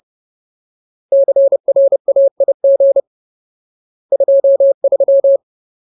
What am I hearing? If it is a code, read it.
Morse code, heard as a keyed tone, about 30 wpm, 550 Hz. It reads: CRAIG 23
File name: Ultimate Word List / 30WPM Ultimate